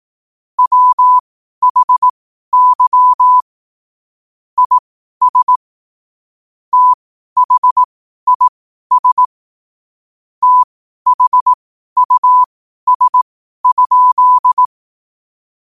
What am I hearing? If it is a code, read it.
Morse code, heard as a keyed tone, about 18 wpm, spaced as for 12 wpm, 1000 Hz. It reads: WHY IS THIS THUS?
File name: qotdfiles / 12wpm